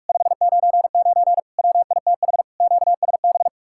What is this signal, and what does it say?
Morse code, heard as a keyed tone, about 59 wpm, 700 Hz. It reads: 599 WITH QSB